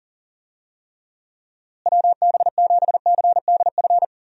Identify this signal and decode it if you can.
Morse code, heard as a keyed tone, about 40 wpm, 700 Hz. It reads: WB7CDF
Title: USA Call Signs - 6 Characters - Repeated 3x - 40wpm